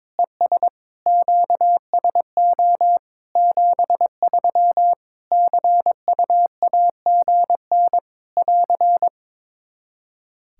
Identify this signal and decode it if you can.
Morse code, heard as a keyed tone, about 22 wpm, 700 Hz. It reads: ES QSO 73 CUAGN <AR>
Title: K1USN SST QSO Practice 22wpm